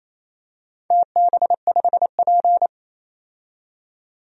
Morse code, heard as a keyed tone, about 28 wpm, 700 Hz. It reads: TB5P
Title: DXCC-100 Call Signs - 4 Characters - 28wpm